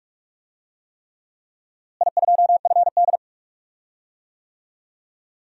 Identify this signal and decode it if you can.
Morse code, heard as a keyed tone, about 46 wpm, 700 Hz. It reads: I2FD
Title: DXCC-100 Call Signs - 4 Characters - Repeated 3x - 45wpm